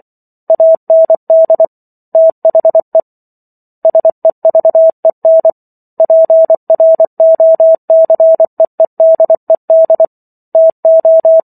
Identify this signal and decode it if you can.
Morse code, heard as a keyed tone, about 24 wpm, 650 Hz. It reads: AND THE SEVEN PROCEEDED TO